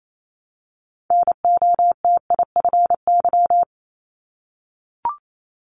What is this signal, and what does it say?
Morse code, heard as a keyed tone, about 28 wpm, 700 Hz. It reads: NOTIFY
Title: Multi-letter Phonemes 28wpm